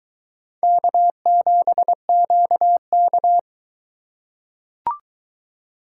Morse code, heard as a keyed tone, about 22 wpm, 700 Hz. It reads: K7QK